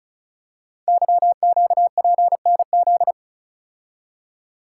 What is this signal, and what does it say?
Morse code, heard as a keyed tone, about 36 wpm, 690 Hz. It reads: YQPNZ
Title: Code Groups - 5 Letters - 35wpm